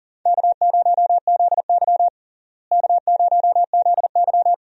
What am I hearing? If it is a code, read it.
Morse code, heard as a keyed tone, about 40 wpm, 700 Hz. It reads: K0ZY K0ZY
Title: USA Call Signs - 4 Characters - Repeated 2x - 40wpm